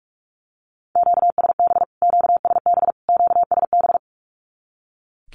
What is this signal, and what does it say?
Morse code, heard as a keyed tone, about 44 wpm, 700 Hz. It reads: QSB QSB QSB
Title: Q-Codes Repeated 3x 45wpm